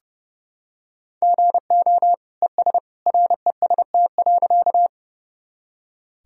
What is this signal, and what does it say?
Morse code, heard as a keyed tone, about 30 wpm, 700 Hz. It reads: GO ES REST.
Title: Sentences from Top 500 Words and Abbreviations 30wpm